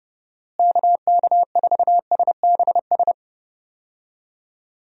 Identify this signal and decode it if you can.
Morse code, heard as a keyed tone, about 32 wpm, 700 Hz. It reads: KK4SBS
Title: USA Call Signs - 6 Characters - Repeated 3x - 30wpm